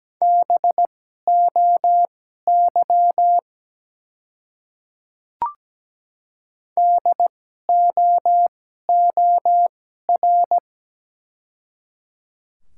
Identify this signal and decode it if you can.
Morse code, heard as a keyed tone, about 17 wpm, spaced as for 12 wpm, 700 Hz. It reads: BOY DOOR
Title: 2x Character Spacing - Top 300 Words - 17wpm